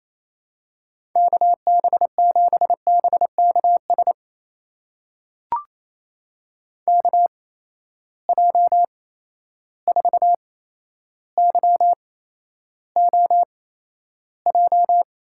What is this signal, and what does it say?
Morse code, heard as a keyed tone, about 28 wpm, spaced as for 12 wpm, 700 Hz. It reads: KB7BKS KJ4YOJ
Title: USA Call Signs - 6 Characters - 8x Character Spacing - 28wpm